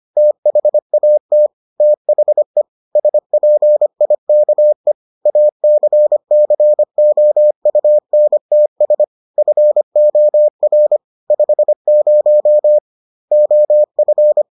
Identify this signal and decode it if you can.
Morse code, heard as a keyed tone, about 25 wpm, 600 Hz. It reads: THAT THE SPIKE ACCOUNTS FOR 50 OF